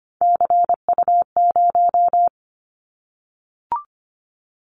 Morse code, heard as a keyed tone, about 25 wpm, 700 Hz. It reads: CU0
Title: Callsign Prefixes for Top 100 DXCC - 25wpm